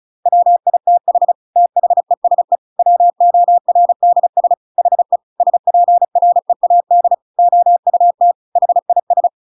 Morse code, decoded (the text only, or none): WITH THESE WORDS HE SPREAD OUT HIS